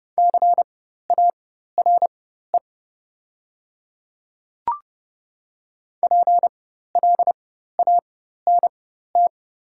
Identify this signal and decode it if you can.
Morse code, heard as a keyed tone, about 30 wpm, spaced as for 14 wpm, 700 Hz. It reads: CARE PLANT
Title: 4x Character Spacing - Top 300 Words - 30wpm